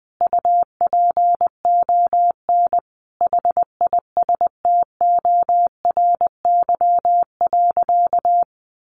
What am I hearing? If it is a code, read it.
Morse code, heard as a keyed tone, about 20 wpm, 700 Hz. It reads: UPON HISTORY.
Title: Sentences from Top 1100 Words 20wpm